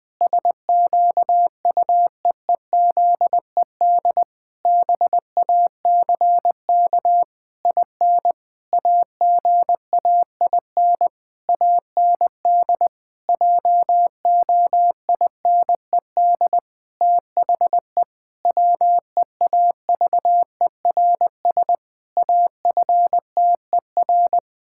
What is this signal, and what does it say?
Morse code, heard as a keyed tone, about 20 wpm, 700 Hz. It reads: SQUEEZED BACK IN AGAIN AND JOINED THE WEAVERS AFTER
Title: Pangram Challenge v2 - 20wpm